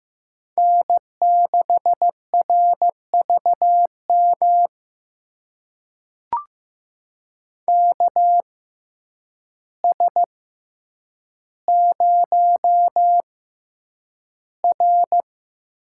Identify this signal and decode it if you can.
Morse code, heard as a keyed tone, about 15 wpm, spaced as for 8 wpm, 700 Hz. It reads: N6RVM KS0R
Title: USA Call Signs - 6x Character Spacing - 15wpm